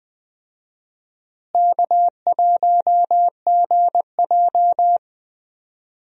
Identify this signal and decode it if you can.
Morse code, heard as a keyed tone, about 20 wpm, 700 Hz. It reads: K1GJ
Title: USA Call Signs - 4 Characters - Repeated 2x - 20wpm